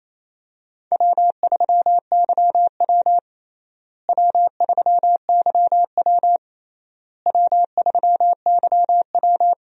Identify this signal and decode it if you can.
Morse code, heard as a keyed tone, about 28 wpm, 700 Hz. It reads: W3YW W3YW W3YW
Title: USA Call Signs - 4 Characters - Repeated 3x - 28wpm